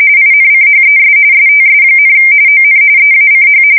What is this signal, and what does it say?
RTTY, about 45.45 baud. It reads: INCHES STS, ON EACH
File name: fsk170 / USS-New-Jersey